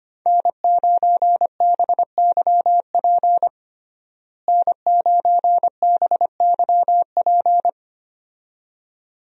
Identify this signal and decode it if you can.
Morse code, heard as a keyed tone, about 25 wpm, 700 Hz. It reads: N9BYP N9BYP
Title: USA Call Signs - 5 Characters - Repeated 2x - 25wpm